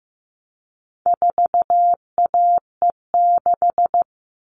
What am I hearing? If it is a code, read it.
Morse code, heard as a keyed tone, about 15 wpm, 700 Hz. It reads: 4AE6
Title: All Cut Numbers - 4-Digit 15wpm